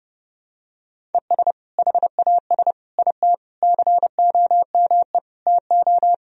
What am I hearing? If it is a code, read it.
Morse code, heard as a keyed tone, about 30 wpm, 700 Hz. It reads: ES HAS IT COME TO
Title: Sentences from Top 100 Words and Abbreviations 30wpm